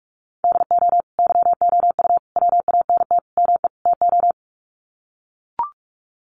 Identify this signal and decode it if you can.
Morse code, heard as a keyed tone, about 44 wpm, 700 Hz. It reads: DO YOU WANT ME TO
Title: Sets of 5 Words 45wpm